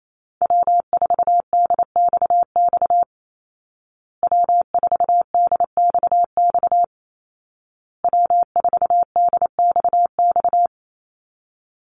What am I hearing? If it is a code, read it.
Morse code, heard as a keyed tone, about 28 wpm, 700 Hz. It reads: W4DXX W4DXX W4DXX
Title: Call Signs Repeated 2x - 28wpm